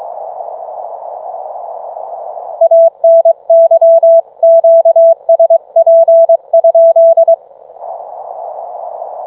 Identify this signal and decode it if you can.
Morse code, heard as a keyed tone, about 22 wpm, 660 Hz. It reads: ANY QSP?